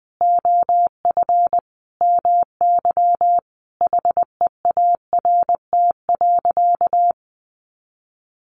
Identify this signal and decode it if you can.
Morse code, heard as a keyed tone, about 20 wpm, 700 Hz. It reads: OF MY HEART.
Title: Sentences from Top 1300 Words 20wpm